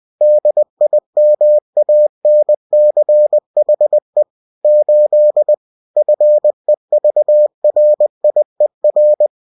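Morse code, heard as a keyed tone, about 20 wpm, 600 Hz. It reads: DIMANCHE 8 FEVRIER
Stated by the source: UFT bulletin sent in CW.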